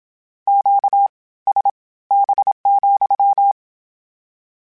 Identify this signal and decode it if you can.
Morse code, heard as a keyed tone, about 27 wpm, spaced as for 18 wpm, 800 Hz. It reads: QSB,
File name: Q Calls / Q Calls-mp3s / Q Calls SPSL2 @25